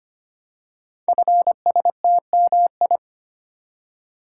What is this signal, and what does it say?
Morse code, heard as a keyed tone, about 25 wpm, 700 Hz. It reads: FSTMI
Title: Code Groups - 5 Letters - 25wpm